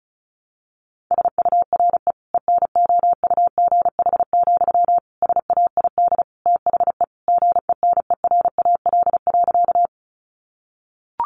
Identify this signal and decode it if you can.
Morse code, heard as a keyed tone, about 34 wpm, 700 Hz. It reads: SURE ENOUGH, SAID THE GENERAL.
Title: Sentences from Top 600 Words 35wpm